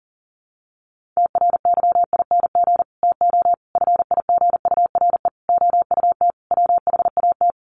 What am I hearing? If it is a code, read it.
Morse code, heard as a keyed tone, about 37 wpm, 700 Hz. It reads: TRYING TO FIGURE OUT WHAT
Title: Sets of 5 Words 40wpm